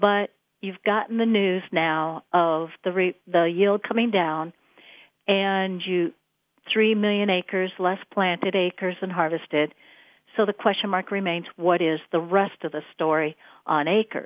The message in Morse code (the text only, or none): none